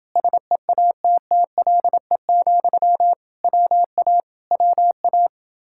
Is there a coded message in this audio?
Yes: Morse code from 0.2 to 5.3 s, heard as a keyed tone, about 29 wpm, 700 Hz. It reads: SEATTLE, WA WA